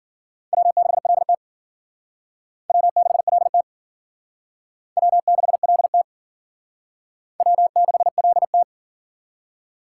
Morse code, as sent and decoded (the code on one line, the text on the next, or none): .-- -.... .-.. - / .-- -.... .-.. - / .-- -.... .-.. - / .-- -.... .-.. -
W6LT W6LT W6LT W6LT